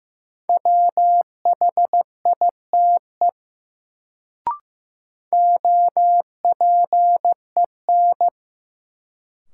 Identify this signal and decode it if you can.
Morse code, heard as a keyed tone, about 16 wpm, 700 Hz. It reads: WHITE OPEN
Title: Sentences from Top 300 Words - New Word Drill - 15wpm